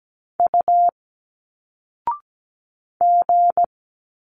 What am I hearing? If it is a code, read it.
Morse code, heard as a keyed tone, about 17 wpm, 700 Hz. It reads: U G
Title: Single Letters 17wpm